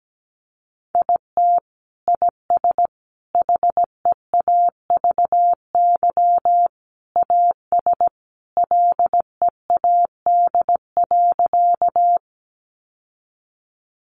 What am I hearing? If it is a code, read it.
Morse code, heard as a keyed tone, about 17 wpm, 700 Hz. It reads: IT IS HEAVY AS LEAD.
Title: Sentences from Top 1100 Words 17wpm